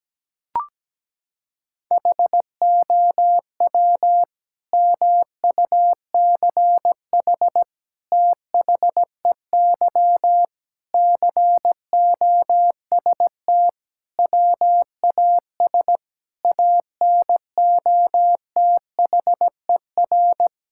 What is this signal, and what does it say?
Morse code, heard as a keyed tone, about 17 wpm, 700 Hz. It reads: HOW MUCH THEY COST WAS ANOTHER
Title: Sentences from Top 1000 Words - Review Part 5 - 17wpm